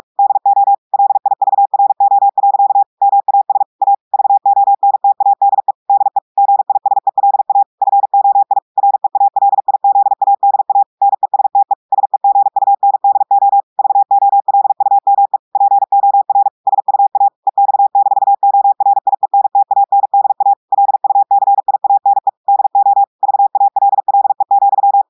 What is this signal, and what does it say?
Morse code, heard as a keyed tone, about 46 wpm, 800 Hz. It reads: DO LIVRO. MAS A VONTADE DE GISELA FOI REALIZADA NESTE SEGUNDO VOLUME POR SUA EX-ORIENTANDA LUCIANE DO VALLE,
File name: portugais texte en clair 1 vitesse 45 mots minutes